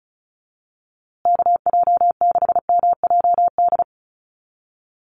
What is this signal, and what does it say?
Morse code, heard as a keyed tone, about 34 wpm, 700 Hz. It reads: KJ6MJD
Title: USA Call Signs Repeated 3x 35wpm